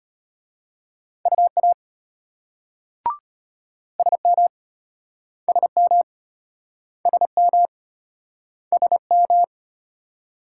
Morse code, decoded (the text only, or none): UA SM SM SM SM